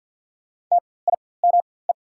Morse code, heard as a keyed tone, about 48 wpm, spaced as for 23 wpm, 690 Hz. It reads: TIME
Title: 4x Character Spacing - Top 100 Words - 50wpm